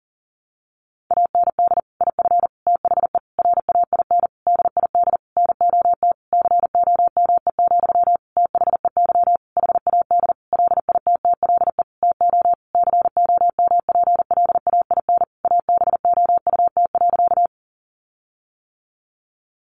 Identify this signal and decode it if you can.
Morse code, heard as a keyed tone, about 37 wpm, 700 Hz. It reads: AND IF THE RAIN DID NOT COME, THEY HAD LITTLE TO COMPLAIN ABOUT.
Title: Sentences from Top 1300 Words 40wpm